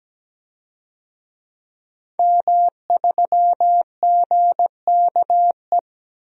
Morse code, heard as a keyed tone, about 17 wpm, 700 Hz. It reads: M3GKE
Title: DXCC-100 Call Signs - 5 Characters - 17wpm